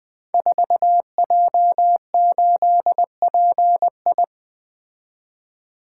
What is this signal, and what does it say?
Morse code, heard as a keyed tone, about 20 wpm, 700 Hz. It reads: 4J8PI
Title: Code Groups - 5 Letter:Numbers - 20wpm